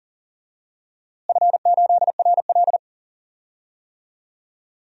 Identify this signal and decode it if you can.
Morse code, heard as a keyed tone, about 40 wpm, 700 Hz. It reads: F8RL
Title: DXCC-100 Call Signs - 4 Characters - Repeated 2x - 40wpm